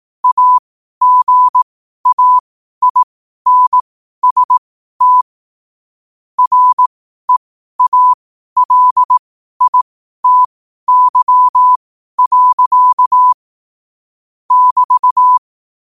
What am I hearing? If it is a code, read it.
Morse code, heard as a keyed tone, about 18 wpm, spaced as for 12 wpm, 1000 Hz. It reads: AGAINST REALITY. <BT>
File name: qotdfiles / 12wpm